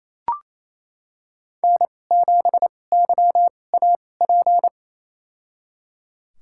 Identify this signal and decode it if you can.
Morse code, heard as a keyed tone, about 28 wpm, spaced as for 20 wpm, 700 Hz. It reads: N7YAP
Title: USA Call Signs - 5 Characters - 2x Character Spacing - 28wpm